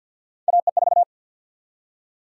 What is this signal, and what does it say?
Morse code, heard as a keyed tone, about 50 wpm, 700 Hz. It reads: AE4
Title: USA Call Sign Prefixes - Repeated 2x - 50wpm